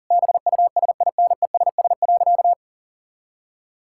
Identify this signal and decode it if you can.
Morse code, heard as a keyed tone, about 37 wpm, 700 Hz. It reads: BUSINESS.
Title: Sentences from Top 1000 Words - Review Part 6 - 40wpm